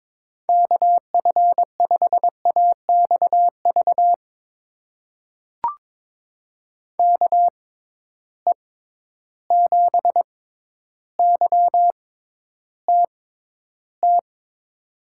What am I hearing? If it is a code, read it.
Morse code, heard as a keyed tone, about 22 wpm, spaced as for 11 wpm, 700 Hz. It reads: KF5AXV KE7YTT
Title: USA Call Signs - 6 Characters - 6x Character Spacing - 22wpm